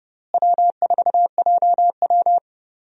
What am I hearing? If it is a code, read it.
Morse code, heard as a keyed tone, about 30 wpm, 700 Hz. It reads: W4JW
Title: USA Call Signs - 4 Characters - Repeated 3x - 30wpm